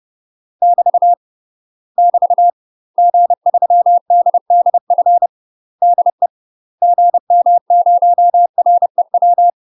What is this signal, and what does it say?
Morse code, heard as a keyed tone, about 30 wpm, 700 Hz. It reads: <BT> <BT> G3DDF DE GM0REW
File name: UK MASTER QSOs TRAINING / 30WPM / UK MASTER QSOs 0034